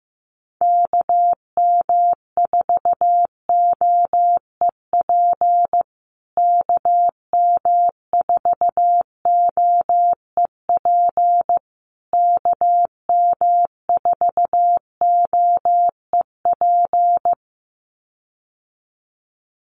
Morse code, heard as a keyed tone, about 15 wpm, 700 Hz. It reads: KM4OEP KM4OEP KM4OEP
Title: USA Call Signs Repeated 3x 15wpm